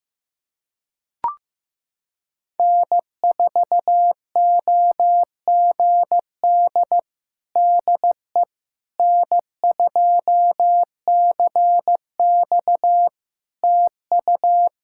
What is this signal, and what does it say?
Morse code, heard as a keyed tone, about 15 wpm, 700 Hz. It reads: N4OGD DE N2CX TU
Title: Practice QSO - Exchange 4 - Ending - 15wpm